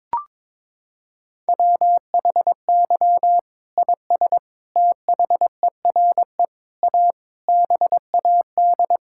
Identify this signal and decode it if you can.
Morse code, heard as a keyed tone, about 22 wpm, 700 Hz. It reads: WHY IS THERE A BAD
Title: Sentences from Top 1600 Words 22wpm